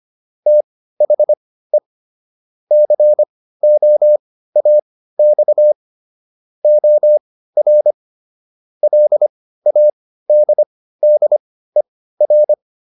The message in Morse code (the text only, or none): THE COAX OR LADDER